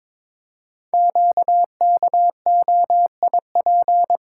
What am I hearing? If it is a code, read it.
Morse code, heard as a keyed tone, about 22 wpm, 700 Hz. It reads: QKOIP
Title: Code Groups - 5 Letters - 22wpm